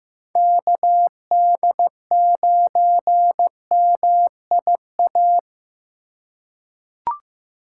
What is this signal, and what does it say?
Morse code, heard as a keyed tone, about 15 wpm, 700 Hz. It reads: KD9MIA
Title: USA Call Signs - 4x Character Spacing - 15wpm